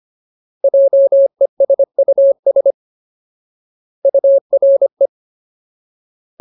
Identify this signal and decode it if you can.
Morse code, heard as a keyed tone, about 25 wpm, 550 Hz. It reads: JESUS URE
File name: Ultimate Word List / 25WPM Ultimate